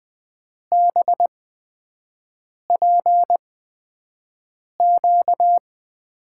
Morse code, heard as a keyed tone, about 20 wpm, spaced as for 5 wpm, 700 Hz. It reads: BPQ
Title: USA Call Signs - 5 Characters - 8x Character Spacing - 20wpm